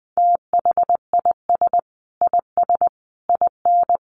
Morse code, heard as a keyed tone, about 20 wpm, 700 Hz. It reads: THIS IS IN
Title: Sentences from Top 1500 Words 20wpm